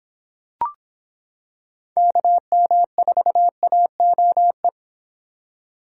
Morse code, heard as a keyed tone, about 25 wpm, 700 Hz. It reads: KM4AOE